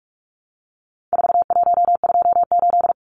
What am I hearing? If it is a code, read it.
Morse code, heard as a keyed tone, about 44 wpm, 700 Hz. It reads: 4128
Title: Numbers - 4-Digit 45wpm